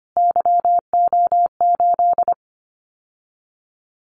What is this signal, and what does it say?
Morse code, heard as a keyed tone, about 25 wpm, 700 Hz. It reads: YO8
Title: Callsign Prefixes for Top 25 DXCC - Repeated 3x - 25wpm